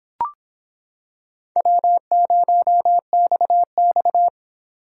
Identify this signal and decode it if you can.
Morse code, heard as a keyed tone, about 25 wpm, 700 Hz. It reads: W0XX